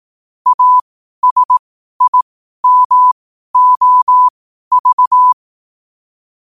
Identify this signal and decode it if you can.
Morse code, heard as a keyed tone, about 18 wpm, spaced as for 12 wpm, 1000 Hz. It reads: ASIMOV
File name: qotdfiles / 12wpm